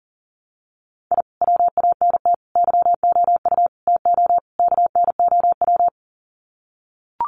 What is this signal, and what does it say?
Morse code, heard as a keyed tone, about 40 wpm, 700 Hz. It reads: I WANT YOU TO KNOW
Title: Sets of 5 Words Encore 40wpm